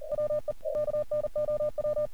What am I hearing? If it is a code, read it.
Morse code, heard as a keyed tone, about 40 wpm, 600 Hz. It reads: WE KNOW
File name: archived / CRSnet